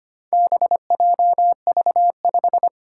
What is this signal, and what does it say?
Morse code, heard as a keyed tone, about 25 wpm, 700 Hz. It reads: BJV5
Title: Four Challenging Letter-Number Combinations 25wpm